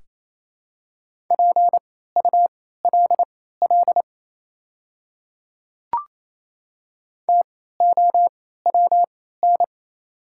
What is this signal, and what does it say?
Morse code, heard as a keyed tone, about 28 wpm, spaced as for 16 wpm, 700 Hz. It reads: PULL TOWN
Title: 3x Character Spacing - Top 500 Words - 28wpm